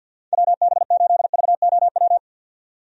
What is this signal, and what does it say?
Morse code, heard as a keyed tone, about 53 wpm, 700 Hz. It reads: WB8VOW
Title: USA Call Signs - 6 Characters - 50wpm